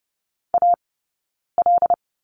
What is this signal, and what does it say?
Morse code, heard as a keyed tone, about 30 wpm, spaced as for 9 wpm, 690 Hz. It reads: AL
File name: course / 30:10wpm / 086 Lesson 15d - Number 5 - Two Characters 30wpm